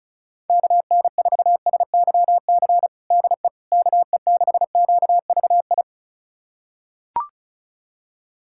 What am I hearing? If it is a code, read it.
Morse code, heard as a keyed tone, about 34 wpm, 700 Hz. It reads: KN4SYC DE KE6QVI
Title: USA Callsign DE Callsign - 6 Characters - 35wpm